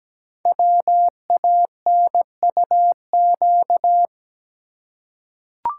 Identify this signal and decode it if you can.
Morse code, heard as a keyed tone, about 17 wpm, 700 Hz. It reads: WANUQ
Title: Code Groups - 5 Letter:Numbers - 17wpm